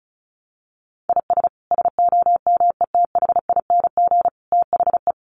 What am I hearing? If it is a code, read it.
Morse code, heard as a keyed tone, about 34 wpm, 700 Hz. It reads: IS SOMETHING THE
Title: Sentences from Top 700 Words 35wpm